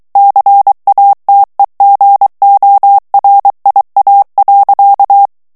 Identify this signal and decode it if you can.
Morse code, heard as a keyed tone, about 23 wpm, 800 Hz. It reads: CATEGORIA.
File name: ITALIEN texte clair 2 vitesse 22 mots minute